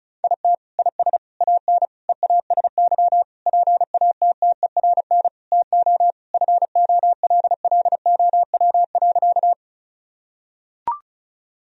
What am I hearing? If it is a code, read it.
Morse code, heard as a keyed tone, about 36 wpm, 700 Hz. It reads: IT IS AN EASY PATTERN TO FOLLOW.
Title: Sentences from Top 1000 Words - Review Part 2 - 35wpm